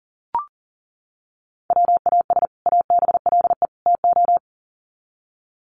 Morse code, heard as a keyed tone, about 40 wpm, 700 Hz. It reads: WAS ABLE TO
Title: Sets of 3 Words 40wpm